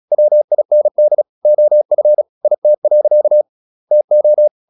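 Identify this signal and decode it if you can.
Morse code, heard as a keyed tone, about 36 wpm, 600 Hz. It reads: WIND OF IT. TO
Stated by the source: Morse practice at 36 wpm, text of Ken Follett's novel Never, from CW med Gnister.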